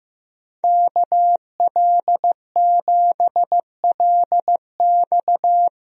Morse code, heard as a keyed tone, about 15 wpm, 700 Hz. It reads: KL7LX
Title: DXCC-100 Call Signs - 5 Characters - 15wpm